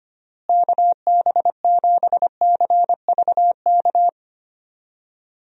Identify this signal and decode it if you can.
Morse code, heard as a keyed tone, about 25 wpm, 700 Hz. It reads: KB7CVK
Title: USA Call Signs - 6 Characters - Repeated 2x - 25wpm